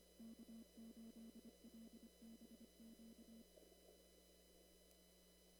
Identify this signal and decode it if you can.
Morse code, heard as a keyed tone, about 25 wpm, 230 Hz. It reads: K8LBQ
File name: CW / kt5j wpx cw-op-k5pi